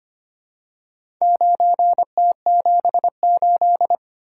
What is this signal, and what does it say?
Morse code, heard as a keyed tone, about 25 wpm, 700 Hz. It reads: 9T78
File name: practice / 25wpm / Four Letter-Number Combinations - 25wpm